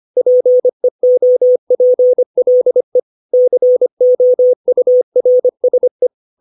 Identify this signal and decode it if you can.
Morse code, heard as a keyed tone, about 25 wpm, 500 Hz. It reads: PEOPLE COURSE
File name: 632 words / 632 Words 25WPM